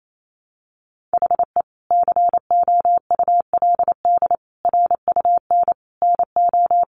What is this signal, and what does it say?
Morse code, heard as a keyed tone, about 28 wpm, 700 Hz. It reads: HE COULD RUN NO
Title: Sentences from Top 300 Words 28wpm